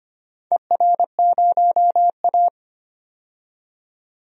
Morse code, heard as a keyed tone, about 25 wpm, 700 Hz. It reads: ER0A
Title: DXCC-100 Call Signs - 4 Characters - 25wpm